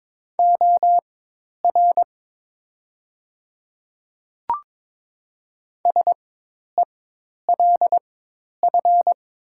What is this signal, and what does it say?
Morse code, heard as a keyed tone, about 22 wpm, spaced as for 10 wpm, 700 Hz. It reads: OR SELF
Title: 4x Character Spacing - Top 500 Words - 22wpm